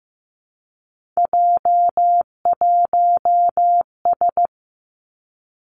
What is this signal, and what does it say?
Morse code, heard as a keyed tone, about 15 wpm, 700 Hz. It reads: J1S
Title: Three Challenging Letter-Number Combinations 15wpm